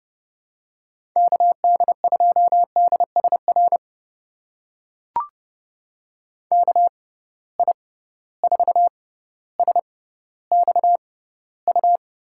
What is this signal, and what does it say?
Morse code, heard as a keyed tone, about 30 wpm, spaced as for 15 wpm, 700 Hz. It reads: KD2DSR KI4SXU
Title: USA Call Signs - 6x Character Spacing - 30wpm